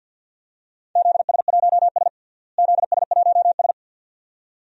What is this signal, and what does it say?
Morse code, heard as a keyed tone, about 52 wpm, 700 Hz. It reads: ZS1S ZS1S
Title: DXCC-100 Call Signs - 4 Characters - Repeated 2x - 50wpm